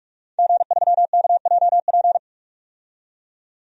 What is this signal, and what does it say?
Morse code, heard as a keyed tone, about 46 wpm, 700 Hz. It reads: G3KJP
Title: DXCC-100 Call Signs - 5 Characters - 45wpm